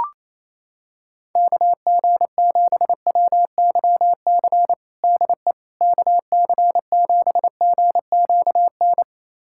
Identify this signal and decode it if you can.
Morse code, heard as a keyed tone, about 27 wpm, 700 Hz. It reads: KG7WYC DE KC7GQN
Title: USA Callsign DE Callsign 28wpm